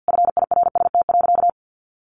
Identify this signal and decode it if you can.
Morse code, heard as a keyed tone, about 50 wpm, 700 Hz. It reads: FIRST.
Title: Sentences from Top 300 Words 50wpm